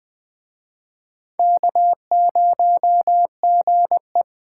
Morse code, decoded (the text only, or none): K0GE